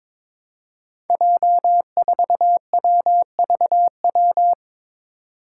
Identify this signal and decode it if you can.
Morse code, heard as a keyed tone, about 21 wpm, 700 Hz. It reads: J4WVW
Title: Five Challenging Letter-Number Combinations 22wpm